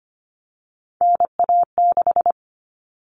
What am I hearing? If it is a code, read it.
Morse code, heard as a keyed tone, about 25 wpm, 700 Hz. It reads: NA6
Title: Common Cut Numbers - 3-Digit 25wpm